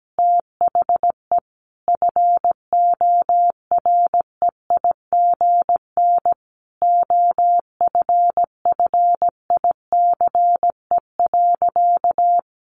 Morse code, heard as a keyed tone, about 17 wpm, 700 Hz. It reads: THE FOREIGN OFFICE.
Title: Sentences from Top 1400 Words 17wpm